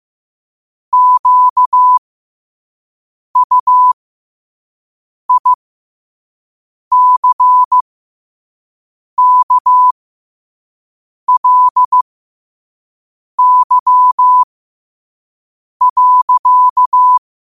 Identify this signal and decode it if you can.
Morse code, heard as a keyed tone, about 15 wpm, spaced as for 5 wpm, 1000 Hz. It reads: QUICKLY.
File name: qotdfiles / 5wpm